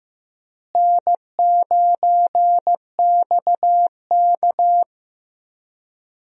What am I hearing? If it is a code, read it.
Morse code, heard as a keyed tone, about 15 wpm, 700 Hz. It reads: N9XK
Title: USA Call Signs - 4 Characters - Repeated 2x - 15wpm